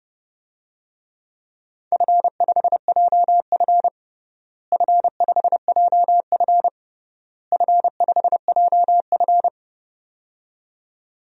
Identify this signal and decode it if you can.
Morse code, heard as a keyed tone, about 32 wpm, 700 Hz. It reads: F5JF F5JF F5JF
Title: DXCC-100 Call Signs - 4 Characters - Repeated 3x - 30wpm